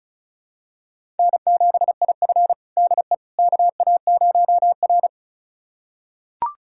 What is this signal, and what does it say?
Morse code, heard as a keyed tone, about 36 wpm, 700 Hz. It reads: N7IF DE KA0R
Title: USA Callsign DE Callsign - 4 Characters - 35wpm